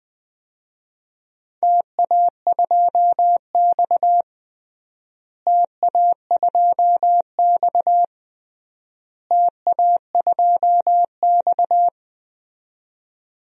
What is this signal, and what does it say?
Morse code, heard as a keyed tone, about 21 wpm, 700 Hz. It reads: TA2X TA2X TA2X
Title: DXCC-100 Call Signs - 4 Characters - Repeated 3x - 20wpm